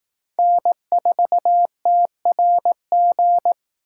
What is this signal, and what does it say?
Morse code, heard as a keyed tone, about 18 wpm, 700 Hz. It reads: N4TRG
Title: Speed Racing - USA 1x3 Call Signs 15wpm